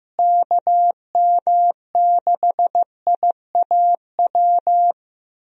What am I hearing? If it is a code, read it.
Morse code, heard as a keyed tone, about 15 wpm, 700 Hz. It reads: KM6IAW
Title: USA Call Signs - 6 Characters - Repeated 2x - 15wpm